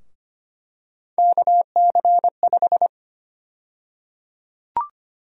Morse code, heard as a keyed tone, about 25 wpm, 700 Hz. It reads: KC5
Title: USA Call Sign Prefixes - 25wpm